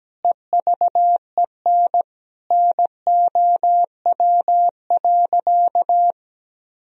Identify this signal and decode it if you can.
Morse code, heard as a keyed tone, about 17 wpm, 700 Hz. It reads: EVEN NOW.
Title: Sentences from Top 1000 Words - Review Part 1 - 17wpm